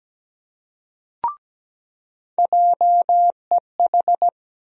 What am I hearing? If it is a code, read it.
Morse code, heard as a keyed tone, about 17 wpm, 700 Hz. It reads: JEH
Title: USA Call Sign Postfixes - 3 Letters 17wpm